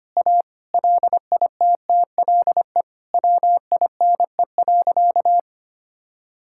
Morse code, heard as a keyed tone, about 25 wpm, 700 Hz. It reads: A LITTLE WINE.
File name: practice / 25wpm / All-Age Appropriate Jokes - 25wpm